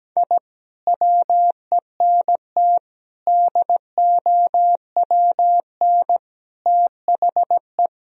Morse code, heard as a keyed tone, about 17 wpm, 700 Hz. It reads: I WENT DOWN THE
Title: Sentences from Top 1000 Words - Review Part 1 - 17wpm